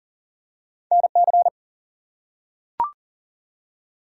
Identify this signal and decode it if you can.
Morse code, heard as a keyed tone, about 41 wpm, 700 Hz. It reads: NC
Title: US States and Canadian Provinces 40wpm